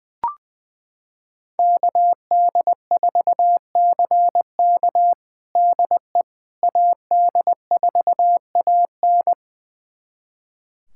Morse code, heard as a keyed tone, about 20 wpm, 700 Hz. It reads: KD4CK DE AD4AN
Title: USA Callsign DE Callsign - 5 Characters - 20wpm